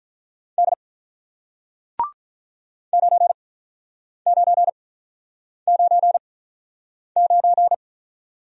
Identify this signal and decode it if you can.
Morse code, heard as a keyed tone, about 48 wpm, spaced as for 9 wpm, 700 Hz. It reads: D 9999